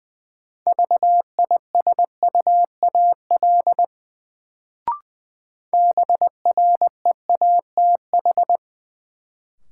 Morse code, heard as a keyed tone, about 20 wpm, 700 Hz. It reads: VISUAL BREATH
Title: Sets of 5 Words Encore - New Word Drill - 20wpm